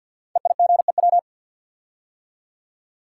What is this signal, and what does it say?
Morse code, heard as a keyed tone, about 47 wpm, 700 Hz. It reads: EIGEW
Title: Code Groups - 5 Letters - 50wpm